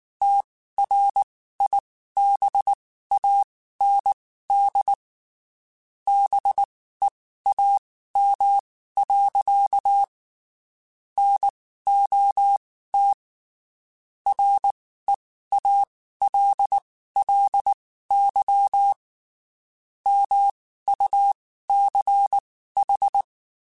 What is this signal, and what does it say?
Morse code, heard as a keyed tone, about 19 wpm, spaced as for 14 wpm, 800 Hz. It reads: TRIBAND BEAM. NOT REALLY MUCH
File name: (spaced) / QSOs Spaced-mp3s / WD0BGP 1 @18 sp